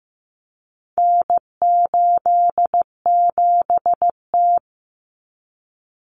Morse code, heard as a keyed tone, about 15 wpm, 700 Hz. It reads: N87T
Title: Common Cut Numbers - 4-Digit 15wpm